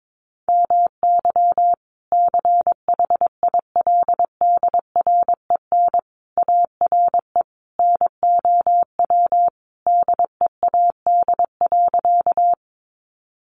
Morse code, heard as a keyed tone, about 22 wpm, 700 Hz. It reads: MY CHILDREN ARE NOW DEAD.
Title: Sentences from Top 1500 Words 22wpm